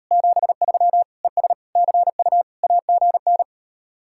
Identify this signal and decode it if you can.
Morse code, heard as a keyed tone, about 39 wpm, 700 Hz. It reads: 73 ES CU AGN